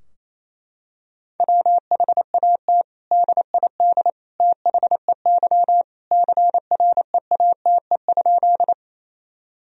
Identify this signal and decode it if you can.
Morse code, heard as a keyed tone, about 28 wpm, 700 Hz. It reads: WHAT DID THEY CREATE?
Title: Sentences from Top 1000 Words - Review Part 3 - 28wpm